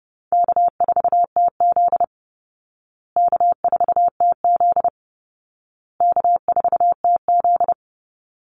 Morse code, heard as a keyed tone, about 30 wpm, 700 Hz. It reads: K4TZ K4TZ K4TZ
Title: Easy Call Signs Repeated 3x - 30wpm